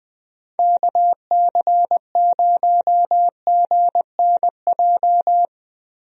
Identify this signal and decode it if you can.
Morse code, heard as a keyed tone, about 20 wpm, 700 Hz. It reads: KC0GNJ
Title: USA Call Signs - 6 Characters - Repeated 3x - 20wpm